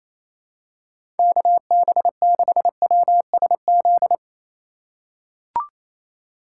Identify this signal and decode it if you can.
Morse code, heard as a keyed tone, about 28 wpm, 700 Hz. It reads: KB6WSZ
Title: USA Call Signs - 6 Characters - 2x Character Spacing - 28wpm